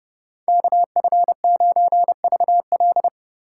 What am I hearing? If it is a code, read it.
Morse code, heard as a keyed tone, about 35 wpm, 700 Hz. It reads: KF9VL
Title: Speed Racing - USA 2x2 Call Signs 30wpm